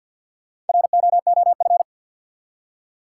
Morse code, heard as a keyed tone, about 47 wpm, 700 Hz. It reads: ROOF
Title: Four Letter Words - 50wpm